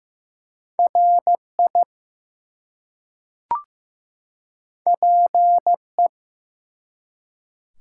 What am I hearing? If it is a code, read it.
Morse code, heard as a keyed tone, about 15 wpm, 700 Hz. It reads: RI PE
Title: US States and Canadian Provinces 15wpm